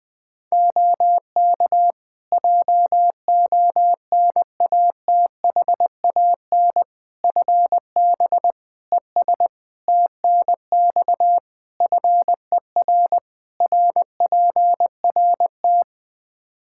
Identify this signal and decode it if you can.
Morse code, heard as a keyed tone, about 20 wpm, 700 Hz. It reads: OK JONATHAN FB ES TNX FER RPRT
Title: Practice QSO - Exchange 2 - Info - 20wpm